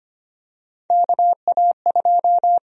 Morse code, heard as a keyed tone, about 25 wpm, 700 Hz. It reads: KA2
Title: USA Call Sign Prefixes - 25wpm